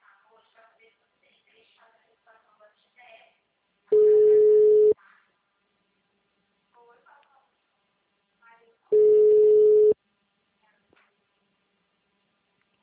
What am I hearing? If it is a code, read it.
Ringback tone (Europe, 425 Hz), with speech.
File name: out- -2028- - - .9202